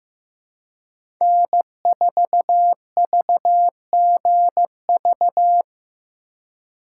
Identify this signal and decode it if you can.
Morse code, heard as a keyed tone, about 15 wpm, 700 Hz. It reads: N4VGV
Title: USA Call Signs - 2x Character Spacing - 15wpm